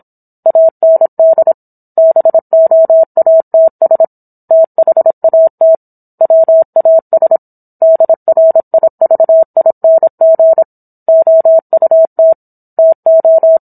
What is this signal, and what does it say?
Morse code, heard as a keyed tone, about 27 wpm, 650 Hz. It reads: AND BOATS THAT WAS DRIVING OUT TO